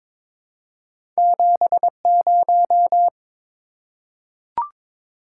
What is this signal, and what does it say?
Morse code, heard as a keyed tone, about 22 wpm, 700 Hz. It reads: 70
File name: practice / 22wpm / Two Letter-Number Combinations - 22wpm